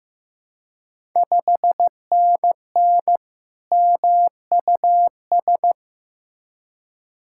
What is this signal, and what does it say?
Morse code, heard as a keyed tone, about 15 wpm, 690 Hz. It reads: 5NN MUS
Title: Ontario QSO Party Exchange 15wpm